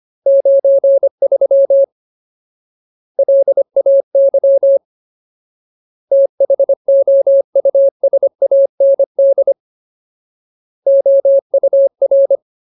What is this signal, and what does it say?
Morse code, heard as a keyed tone, about 25 wpm, 550 Hz. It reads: 93 LAY THOUSAND OUR
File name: Ultimate Word List / 25WPM Ultimate